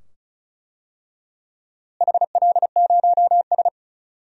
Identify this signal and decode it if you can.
Morse code, heard as a keyed tone, about 36 wpm, 700 Hz. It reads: HL0S
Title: DXCC-100 Call Signs - 4 Characters - Repeated 2x - 35wpm